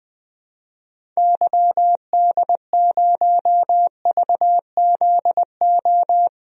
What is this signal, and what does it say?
Morse code, heard as a keyed tone, about 20 wpm, 700 Hz. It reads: YD0VZO
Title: DXCC-100 Call Signs - 6 Characters - 20wpm